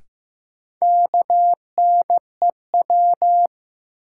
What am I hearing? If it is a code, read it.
Morse code, heard as a keyed tone, about 15 wpm, 700 Hz. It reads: KNEW
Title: Sentences from Top 400 Words - New Word Drill - 15wpm